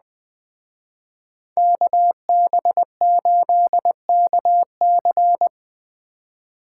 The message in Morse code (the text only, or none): KB8KC